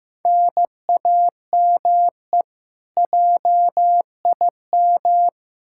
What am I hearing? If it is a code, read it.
Morse code, heard as a keyed tone, about 15 wpm, 700 Hz. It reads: NAME JIM
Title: K1USN SST QSO Practice 15wpm